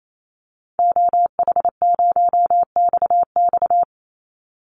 Morse code, heard as a keyed tone, about 28 wpm, 700 Hz. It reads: OH0XX
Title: Call Signs Repeated 3x - 28wpm